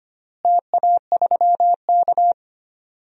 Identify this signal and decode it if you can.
Morse code, heard as a keyed tone, about 25 wpm, 700 Hz. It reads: TA3K
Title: DXCC-100 Call Signs - 4 Characters - Repeated 3x - 25wpm